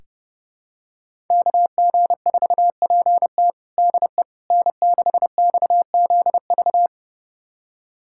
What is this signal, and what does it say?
Morse code, heard as a keyed tone, about 31 wpm, 700 Hz. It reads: KG4PT DE N6XZV
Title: USA Callsign DE Callsign - 5 Characters - 30wpm